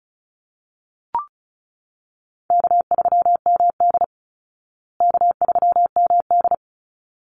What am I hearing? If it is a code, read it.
Morse code, heard as a keyed tone, about 34 wpm, 700 Hz. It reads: K3MD K3MD
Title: Call Signs Repeated 3x - 35wpm